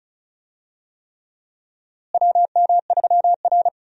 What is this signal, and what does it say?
Morse code, heard as a keyed tone, about 34 wpm, 700 Hz. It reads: WM3R
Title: USA Call Signs - 4 Characters - Repeated 3x - 35wpm